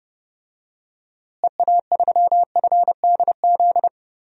Morse code, heard as a keyed tone, about 30 wpm, 700 Hz. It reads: EA3FDZ
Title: DXCC-100 Call Signs - 6 Characters - Repeated 3x - 30wpm